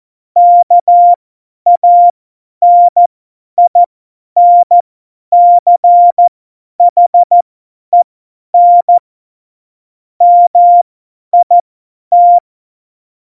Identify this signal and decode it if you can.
Morse code, heard as a keyed tone, about 14 wpm, spaced as for 10 wpm, 700 Hz. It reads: KANINCHEN MIT
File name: Morsekurs / Witze / Witz 22. Januar 2019, 50 F70